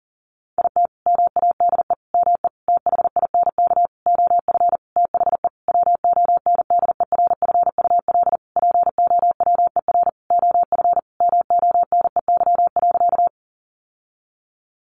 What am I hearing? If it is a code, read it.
Morse code, heard as a keyed tone, about 40 wpm, 700 Hz. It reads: IT MADE ME THINK OF THE WONDERFUL POWER OF MONEY.
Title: Sentences from Top 1100 Words 40wpm